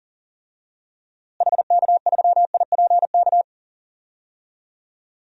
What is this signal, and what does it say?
Morse code, heard as a keyed tone, about 40 wpm, 700 Hz. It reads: HK3IPK
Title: DXCC-100 Call Signs - 6 Characters - Repeated 3x - 40wpm